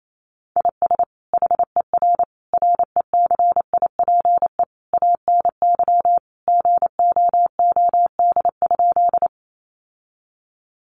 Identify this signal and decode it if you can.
Morse code, heard as a keyed tone, about 28 wpm, 700 Hz. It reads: IS HER RECIPE ANY GOOD?
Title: Sentences from Top 1200 Words 28wpm